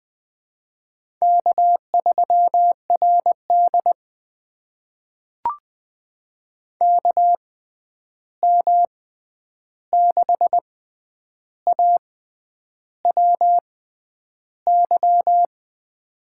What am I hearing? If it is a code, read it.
Morse code, heard as a keyed tone, about 20 wpm, spaced as for 9 wpm, 700 Hz. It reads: K3RD KM6AWY
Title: USA Call Signs - 6x Character Spacing - 20wpm